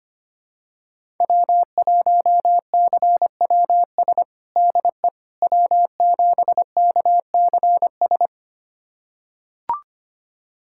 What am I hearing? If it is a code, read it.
Morse code, heard as a keyed tone, about 25 wpm, 700 Hz. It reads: W1CWS DE W7KCS
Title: USA Callsign DE Callsign - 5 Characters - 25wpm